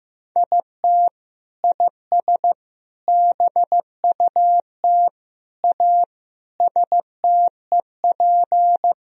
Morse code, heard as a keyed tone, about 16 wpm, 700 Hz. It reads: IT IS BUT A STEP
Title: Sentences from Top 1000 Words - Review Part 2 - 15wpm